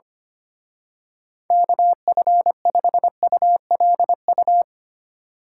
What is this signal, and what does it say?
Morse code, heard as a keyed tone, about 27 wpm, 700 Hz. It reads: KF5ULU
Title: Speed Racing - USA 2x3 Call Signs 25wpm